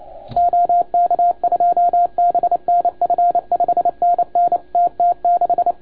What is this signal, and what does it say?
Morse code, heard as a keyed tone, about 29 wpm, 690 Hz. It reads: OK2BNF 5NN TT6
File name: Archiv / CUC287 / OK1FGD CUC 287